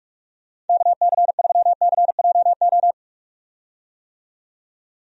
Morse code, heard as a keyed tone, about 44 wpm, 700 Hz. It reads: KC3CJO
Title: USA Call Signs - 6 Characters - 45wpm